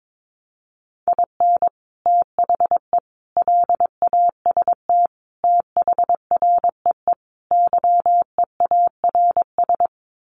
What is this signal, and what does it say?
Morse code, heard as a keyed tone, about 22 wpm, 700 Hz. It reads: IN THE LAST THREE YEARS
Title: Sets of 5 Words Encore2 22wpm